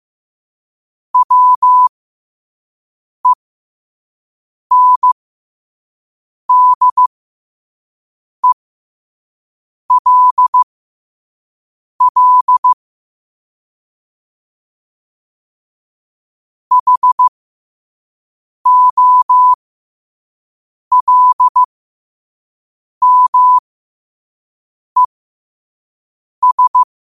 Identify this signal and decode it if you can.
Morse code, heard as a keyed tone, about 15 wpm, spaced as for 5 wpm, 1000 Hz. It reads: WENDELL HOLMES
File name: qotdfiles / 5wpm